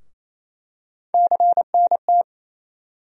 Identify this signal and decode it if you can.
Morse code, heard as a keyed tone, about 28 wpm, 700 Hz. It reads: CNT
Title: USA Call Sign Postfixes - 3 Letters 28wpm